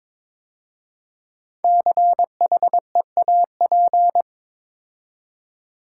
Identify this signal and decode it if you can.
Morse code, heard as a keyed tone, about 22 wpm, 700 Hz. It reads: CHEAP
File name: 5 Letter Words - First 3 Letters Given - 22wpm